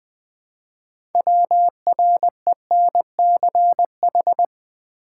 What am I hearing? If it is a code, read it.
Morse code, heard as a keyed tone, about 20 wpm, 700 Hz. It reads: WRENCH
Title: Easy Words with Hints - 20wpm